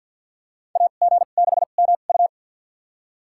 Morse code, heard as a keyed tone, about 50 wpm, spaced as for 36 wpm, 700 Hz. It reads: AG6MU
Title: USA Call Signs - 5 Characters - 2x Character Spacing - 50wpm